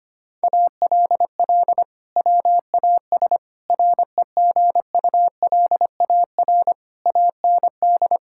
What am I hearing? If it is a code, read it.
Morse code, heard as a keyed tone, about 25 wpm, 700 Hz. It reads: ALL WAS REGULAR AND
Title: Sentences from Top 1000 Words - Review Part 6 - 25wpm